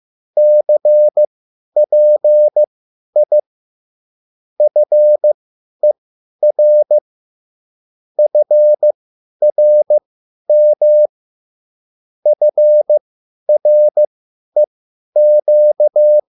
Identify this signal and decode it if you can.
Morse code, heard as a keyed tone, about 15 wpm, spaced as for 10 wpm, 600 Hz. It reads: CPI FER FRM FREQ